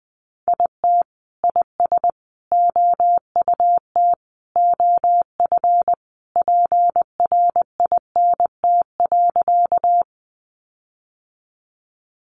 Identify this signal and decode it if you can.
Morse code, heard as a keyed tone, about 20 wpm, 700 Hz. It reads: IT IS OUT OF PRINT.
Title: Sentences from Top 1400 Words 20wpm